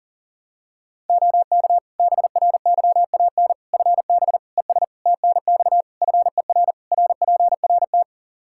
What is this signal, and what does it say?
Morse code, heard as a keyed tone, about 40 wpm, 700 Hz. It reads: OK BRYAN FB ES TNX FER RPRT
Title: Practice QSO - Exchange 2 - Info - 40wpm